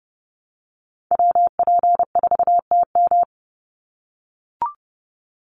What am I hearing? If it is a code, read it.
Morse code, heard as a keyed tone, about 30 wpm, 700 Hz. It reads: WP4TM
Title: USA Call Signs 30wpm